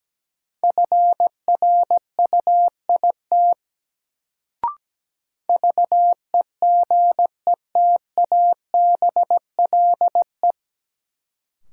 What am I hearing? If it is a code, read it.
Morse code, heard as a keyed tone, about 17 wpm, 700 Hz. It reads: FRUIT VEGETABLE
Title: Sentences from Top 1300 Words - New Word Drill - 17wpm